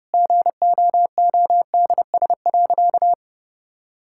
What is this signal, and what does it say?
Morse code, heard as a keyed tone, about 32 wpm, 700 Hz. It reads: GOODS.
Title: Sentences from Top 1000 Words - Review Part 6 - 30wpm